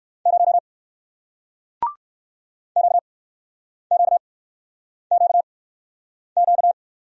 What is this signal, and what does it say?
Morse code, heard as a keyed tone, about 53 wpm, spaced as for 9 wpm, 700 Hz. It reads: 0 QQQQ
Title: Speed-Racing Single Letter-Number 45wpm